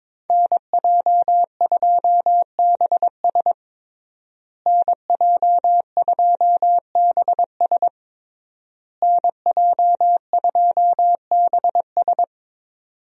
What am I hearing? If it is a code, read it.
Morse code, heard as a keyed tone, about 22 wpm, 700 Hz. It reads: NJ2BS NJ2BS NJ2BS
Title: USA Call Signs - 5 Characters - Repeated 3x - 22wpm